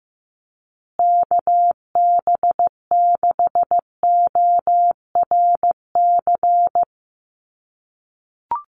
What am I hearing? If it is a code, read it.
Morse code, heard as a keyed tone, about 15 wpm, 700 Hz. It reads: KB6ORC
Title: USA Call Signs Repeated 2x 15wpm